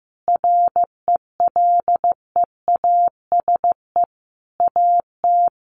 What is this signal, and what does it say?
Morse code, heard as a keyed tone, about 15 wpm, 700 Hz. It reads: RELEASE AT
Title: Sentences from Top 1100 Words 15wpm